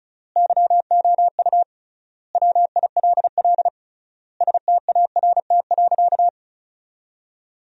Morse code, heard as a keyed tone, about 36 wpm, 700 Hz. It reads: YOU WILL START.
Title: 3x Word-Spacing - Sentences from Top 300 Words - 35wpm